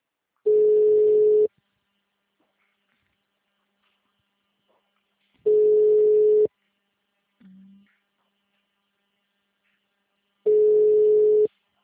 Ringback tone (Europe, 425 Hz).
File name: out- -2007- - - .4